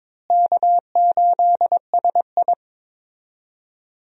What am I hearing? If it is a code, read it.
Morse code, heard as a keyed tone, about 22 wpm, 700 Hz. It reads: K8SI
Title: USA Call Signs - 4 Characters - Repeated 3x - 22wpm